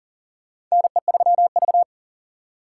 Morse code, heard as a keyed tone, about 40 wpm, 700 Hz. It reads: NE3V
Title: USA Call Signs - 4 Characters - 2x Character Spacing - 40wpm